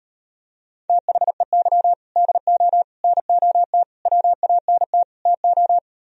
Morse code, heard as a keyed tone, about 38 wpm, 700 Hz. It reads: THEY DO NOT WANT TO